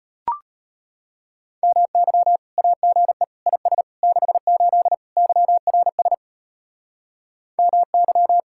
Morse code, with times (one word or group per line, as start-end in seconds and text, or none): 1.6-2.4 MY
2.6-3.2 AGE
3.5-3.8 IS
4.0-4.9 68
5.2-6.1 YRS
7.6-8.4 MY